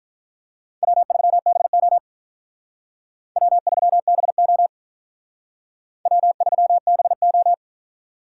Morse code, heard as a keyed tone, about 43 wpm, 700 Hz. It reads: W3BO W3BO W3BO
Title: Speed Racing - USA Call Signs 35wpm